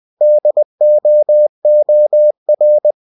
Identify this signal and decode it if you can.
Morse code, heard as a keyed tone, about 20 wpm, 600 Hz. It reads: DOOR
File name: cw-books / winnie-the-pooh / winnie-20wpm-page-0018